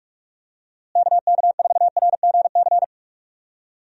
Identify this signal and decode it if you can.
Morse code, heard as a keyed tone, about 45 wpm, 700 Hz. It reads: KK4RGC
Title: USA Call Signs - 6 Characters - Repeated 3x - 45wpm